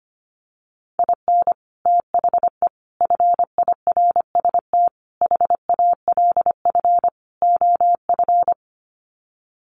Morse code, heard as a keyed tone, about 25 wpm, 700 Hz. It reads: IN THE FIRST HALF OF
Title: Sets of 5 Words 25wpm